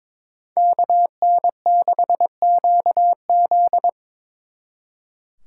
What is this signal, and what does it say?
Morse code, heard as a keyed tone, about 26 wpm, 700 Hz. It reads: KN6QZ
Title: Speed Racing - USA 2x2 Call Signs 22wpm